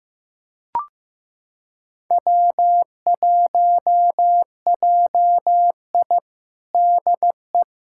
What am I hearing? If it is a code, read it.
Morse code, heard as a keyed tone, about 15 wpm, 700 Hz. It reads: W1JI DE
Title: USA Callsign DE Callsign - 4 Characters - 15wpm